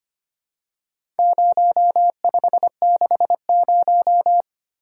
Morse code, heard as a keyed tone, about 25 wpm, 700 Hz. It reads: 0560
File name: practice / 25wpm / Four Letter-Number Combinations - 25wpm